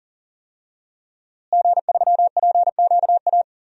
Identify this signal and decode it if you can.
Morse code, heard as a keyed tone, about 40 wpm, 700 Hz. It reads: G3PQA
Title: DXCC-100 Call Signs - 5 Characters - Repeated 2x - 40wpm